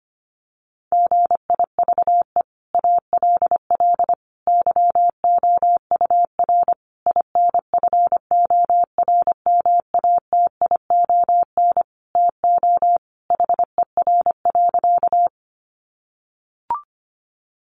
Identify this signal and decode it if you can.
Morse code, heard as a keyed tone, about 25 wpm, 700 Hz. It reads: GIVE ALL YOUR INFORMATION TO HER.
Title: Sentences from Top 600 Words 25wpm